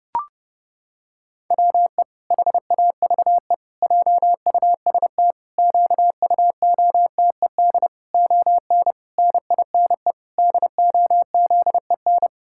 Morse code, heard as a keyed tone, about 32 wpm, 700 Hz. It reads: WE HAVE JUST QUOTED ON NINE DOZEN
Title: Pangram Challenge v2 - 30wpm